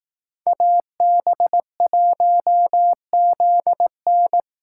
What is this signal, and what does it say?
Morse code, heard as a keyed tone, about 18 wpm, 700 Hz. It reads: AB1ZN
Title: Speed Racing - USA 2x2 Call Signs 15wpm